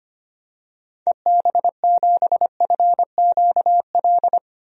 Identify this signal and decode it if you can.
Morse code, heard as a keyed tone, about 25 wpm, 700 Hz. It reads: EB7FQL
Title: DXCC-100 Call Signs - 6 Characters - 25wpm